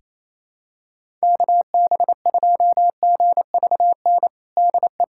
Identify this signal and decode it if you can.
Morse code, heard as a keyed tone, about 28 wpm, 700 Hz. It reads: KB2GVN DE
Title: USA Callsign DE Callsign - 6 Characters - 28wpm